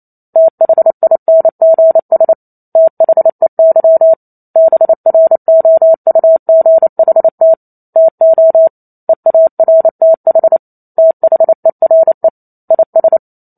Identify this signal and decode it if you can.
Morse code, heard as a keyed tone, about 28 wpm, 650 Hz. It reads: THINGS THEY BROUGHT TO EARTH THERE IS
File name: Chapter 2 - 28 WPM